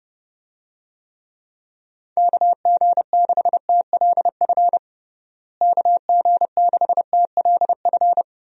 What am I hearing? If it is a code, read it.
Morse code, heard as a keyed tone, about 32 wpm, 700 Hz. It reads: KG6TLF KG6TLF
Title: USA Call Signs - 6 Characters - Repeated 3x - 30wpm